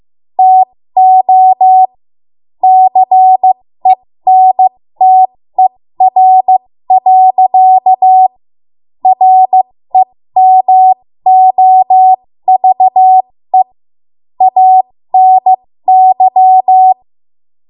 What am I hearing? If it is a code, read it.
Morse code, heard as a keyed tone, about 15 wpm, spaced as for 13 wpm, 750 Hz. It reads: TO CENTER. REMOVE ANY
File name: Morse / 13WPM